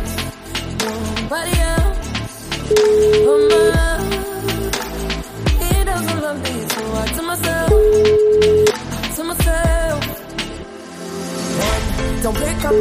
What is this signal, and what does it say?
Ringback tone (Europe, 425 Hz), with speech.